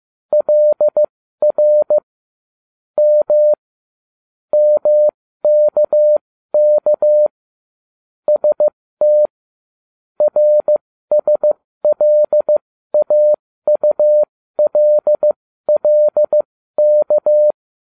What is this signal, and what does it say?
Morse code, heard as a keyed tone, about 15 wpm, spaced as for 13 wpm, 600 Hz. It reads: LR M MKK ST RSLAULLK